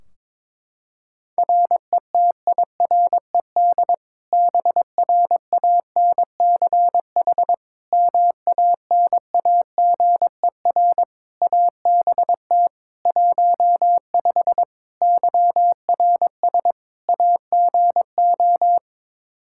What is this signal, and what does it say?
Morse code, heard as a keyed tone, about 22 wpm, 700 Hz. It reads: RETIRED BRANCH MANAGER ABT 15 YRS AGO
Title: Practice QSO - Exchange 3 - More Info - 22wpm